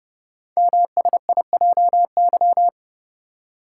Morse code, heard as a keyed tone, about 30 wpm, 700 Hz. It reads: MSIJY
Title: Code Groups - 5 Letters - 30wpm